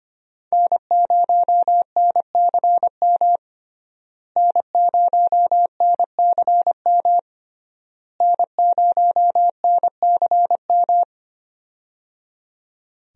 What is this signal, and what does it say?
Morse code, heard as a keyed tone, about 25 wpm, 700 Hz. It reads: N0NCM N0NCM N0NCM
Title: USA Call Signs - 5 Characters - Repeated 3x - 25wpm